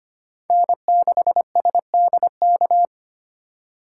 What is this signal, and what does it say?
Morse code, heard as a keyed tone, about 25 wpm, 700 Hz. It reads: N6SDK
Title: USA Call Signs - 5 Characters - 3x Character Spacing - 25wpm